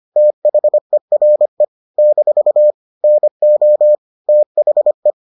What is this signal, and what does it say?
Morse code, heard as a keyed tone, about 25 wpm, 600 Hz. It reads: THERE - NO THE